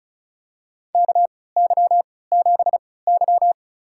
Morse code, heard as a keyed tone, about 34 wpm, spaced as for 20 wpm, 700 Hz. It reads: KY7Y
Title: USA Call Signs - 4 Characters - 3x Character Spacing - 35wpm